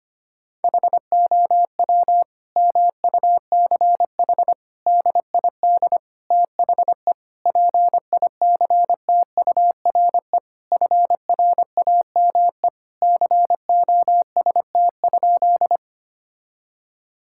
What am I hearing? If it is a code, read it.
Morse code, heard as a keyed tone, about 25 wpm, 700 Hz. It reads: HOW MUCH DID THE PICTURE FRAME COST?
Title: Sentences from Top 1000 Words - Review Part 6 - 25wpm